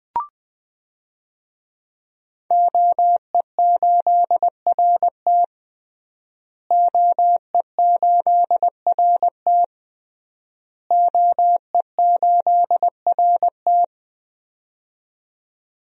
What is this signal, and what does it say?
Morse code, heard as a keyed tone, about 20 wpm, 700 Hz. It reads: OE8RT OE8RT OE8RT
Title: DXCC-100 Call Signs - 5 Characters - Repeated 3x - 20wpm